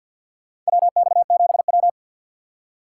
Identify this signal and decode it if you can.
Morse code, heard as a keyed tone, about 47 wpm, 700 Hz. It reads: WX7W
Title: USA Call Signs - 4 Characters - 2x Character Spacing - 50wpm